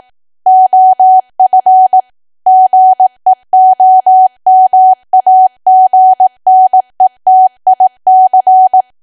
Morse code, heard as a keyed tone, about 18 wpm, 750 Hz. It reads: OF GEOMAGNETIC
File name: cwa / W1AW-Level2 / 18WPM